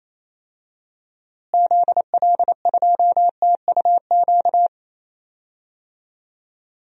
Morse code, heard as a keyed tone, about 28 wpm, 700 Hz. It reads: ZL2TUQ
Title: DXCC-100 Call Signs - 6 Characters - Repeated 2x - 28wpm